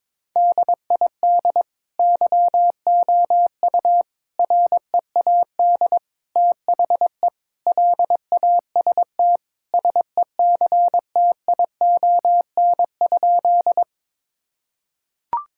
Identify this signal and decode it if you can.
Morse code, heard as a keyed tone, about 22 wpm, 700 Hz. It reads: DID YOU READ THE LAST SECTION?
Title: Sentences from Top 1000 Words - Review Part 5 - 22wpm